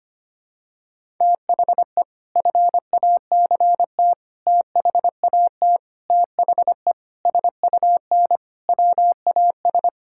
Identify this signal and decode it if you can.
Morse code, heard as a keyed tone, about 25 wpm, 700 Hz. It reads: THE FACT THAT THE SUN WAS
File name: practice / 25wpm / Sentences with 4-Word Phrases - 25wpm